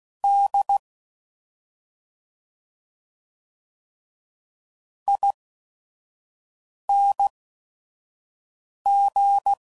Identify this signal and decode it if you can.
Morse code, heard as a keyed tone, about 16 wpm, spaced as for 5 wpm, 800 Hz. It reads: D ING